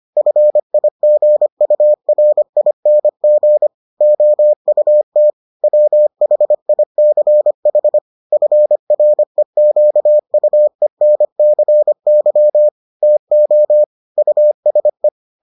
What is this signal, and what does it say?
Morse code, heard as a keyed tone, about 25 wpm, 600 Hz. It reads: FIGURING OUT WHICH FREQUENCY TO USE